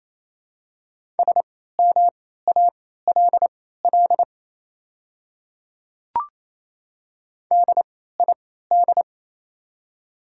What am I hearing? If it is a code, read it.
Morse code, heard as a keyed tone, about 28 wpm, spaced as for 16 wpm, 700 Hz. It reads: SMALL DID
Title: 3x Character Spacing - Top 500 Words - 28wpm